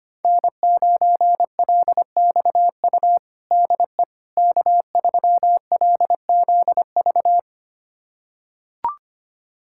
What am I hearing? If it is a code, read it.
Morse code, heard as a keyed tone, about 25 wpm, 700 Hz. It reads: N9LXU DE K3LZV
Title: USA Callsign DE Callsign - 5 Characters - 25wpm